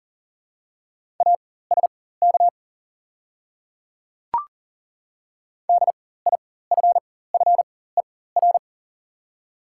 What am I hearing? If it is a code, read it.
Morse code, heard as a keyed tone, about 40 wpm, spaced as for 19 wpm, 690 Hz. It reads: ASK DIFFER
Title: 4x Character Spacing - Top 200 Words - 40wpm